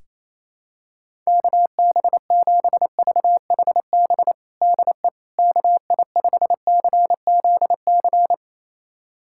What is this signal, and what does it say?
Morse code, heard as a keyed tone, about 28 wpm, 700 Hz. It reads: KB7VHB DE KI5CZC
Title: USA Callsign DE Callsign - 6 Characters - 28wpm